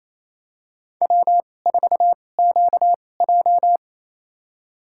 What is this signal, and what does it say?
Morse code, heard as a keyed tone, about 28 wpm, spaced as for 20 wpm, 700 Hz. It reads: W4QJ
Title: USA Call Signs - 2x Character Spacing - 28wpm